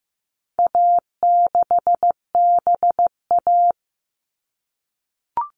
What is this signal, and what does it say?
Morse code, heard as a keyed tone, about 15 wpm, 700 Hz. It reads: A6BA
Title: All Cut Numbers - 4-Digit 15wpm